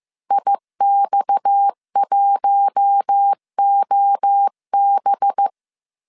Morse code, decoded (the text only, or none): IX1OB